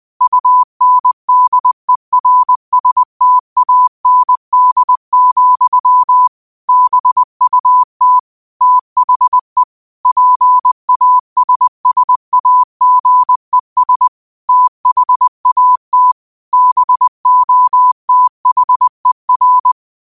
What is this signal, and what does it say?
Morse code, heard as a keyed tone, about 20 wpm, 1000 Hz. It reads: UNDERSTAND, BUT THE PASSAGES THAT BOTHER
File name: qotdfiles / 20wpm